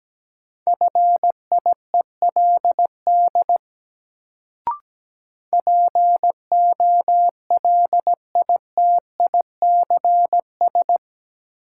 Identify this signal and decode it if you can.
Morse code, heard as a keyed tone, about 17 wpm, 700 Hz. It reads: FIELD POLITICS
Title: Sentences from Top 1000 Words - New Word Drill - 17wpm